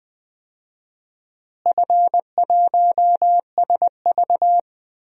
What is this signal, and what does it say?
Morse code, heard as a keyed tone, about 20 wpm, 700 Hz. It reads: F1SV
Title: DXCC-100 Call Signs - 4 Characters - Repeated 2x - 20wpm